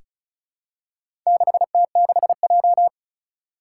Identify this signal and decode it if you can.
Morse code, heard as a keyed tone, about 36 wpm, 700 Hz. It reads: 6T6J
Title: Four Challenging Letter-Number Combinations 35wpm